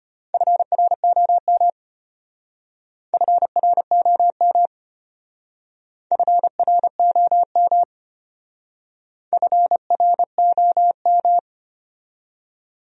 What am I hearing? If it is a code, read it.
Morse code, heard as a keyed tone, about 31 wpm, 700 Hz. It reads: FROM FROM FROM FROM